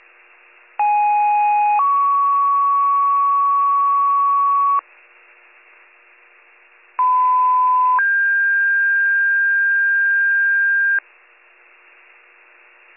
Two-tone page, heard as a one-second tone, then a three-second tone, about 830 then 1120 Hz, about 1010 then 1600 Hz. It is heard from 0.8 to 11.0 s.